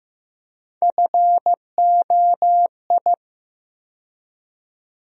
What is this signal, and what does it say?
Morse code, heard as a keyed tone, about 15 wpm, 700 Hz. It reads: FOI
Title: USA Call Sign Postfixes - 3 Letters 15wpm